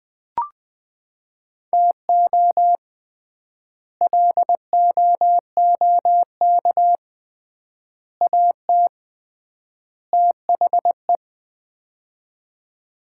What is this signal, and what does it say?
Morse code, heard as a keyed tone, about 20 wpm, 700 Hz. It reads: TO LOOK AT THE
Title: 3x Word-Spacing - Sets of 4 Words - 20wpm